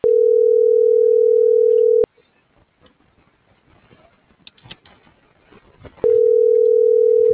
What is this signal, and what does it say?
Ringback tone (North America), 440 + 480 Hz.